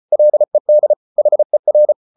Morse code, heard as a keyed tone, about 34 wpm, 600 Hz. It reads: LED HER